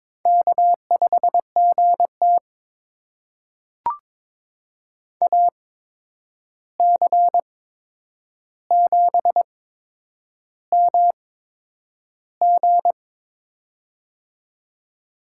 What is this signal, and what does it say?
Morse code, heard as a keyed tone, about 22 wpm, spaced as for 9 wpm, 700 Hz. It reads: K5GT AC7MG
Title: USA Call Signs - 8x Character Spacing - 22wpm